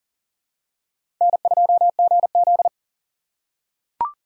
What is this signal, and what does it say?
Morse code, heard as a keyed tone, about 40 wpm, 700 Hz. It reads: N2GZ
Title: USA Call Signs - 4 Characters - 2x Character Spacing - 40wpm